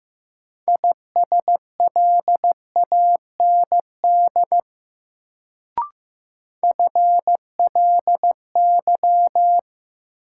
Morse code, heard as a keyed tone, about 15 wpm, 700 Hz. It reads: ISLAND FLY
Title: Sentences from Top 500 Words - New Word Drill - 15wpm